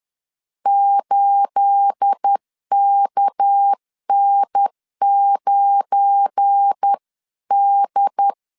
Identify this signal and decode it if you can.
Morse code, heard as a keyed tone, about 10 wpm, 800 Hz. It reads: 8KN9D
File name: CW / DET / 1 9 1500C S A 10wpm 4511